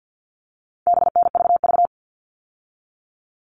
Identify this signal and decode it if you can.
Morse code, heard as a keyed tone, about 50 wpm, 700 Hz. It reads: BNVV
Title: All Cut Numbers - 4-Digit 50wpm